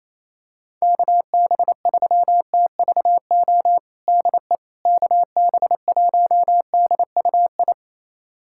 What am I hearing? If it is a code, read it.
Morse code, heard as a keyed tone, about 27 wpm, 700 Hz. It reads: KB3TVO DE KB1DUI
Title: USA Callsign DE Callsign - 6 Characters - 28wpm